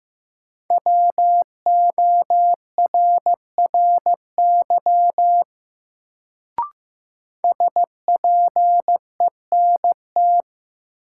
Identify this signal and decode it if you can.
Morse code, heard as a keyed tone, about 15 wpm, 700 Hz. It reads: WORRY SPENT
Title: Sets of 5 Words - New Word Drill - 15wpm